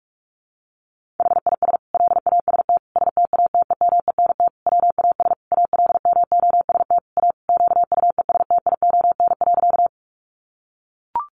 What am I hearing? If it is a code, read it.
Morse code, heard as a keyed tone, about 44 wpm, 700 Hz. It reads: HIS LAST STATEMENT WAS ALMOST A QUESTION.
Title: Sentences from Top 1200 Words 45wpm